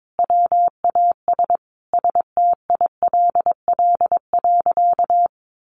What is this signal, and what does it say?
Morse code, heard as a keyed tone, about 22 wpm, 700 Hz. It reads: WAS STILL.
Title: Sentences from Top 700 Words 22wpm